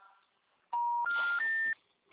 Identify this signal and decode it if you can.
Special information tone.